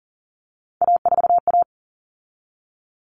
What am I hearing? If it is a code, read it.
Morse code, heard as a keyed tone, about 40 wpm, 700 Hz. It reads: A4A
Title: All Cut Numbers - 3-Digit 40wpm